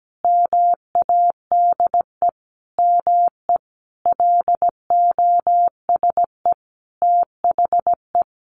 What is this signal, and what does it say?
Morse code, heard as a keyed tone, about 17 wpm, 700 Hz. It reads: MADE ME LOSE THE
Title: Sentences from Top 1300 Words 17wpm